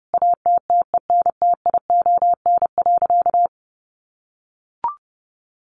Morse code, heard as a keyed tone, about 30 wpm, 700 Hz. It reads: ATTENTION.
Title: Sentences from Top 1400 Words 30wpm